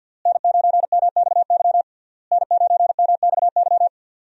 Morse code, heard as a keyed tone, about 49 wpm, 690 Hz. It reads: N9MXY N9MXY
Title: USA Call Signs - 5 Characters - Repeated 3x - 50wpm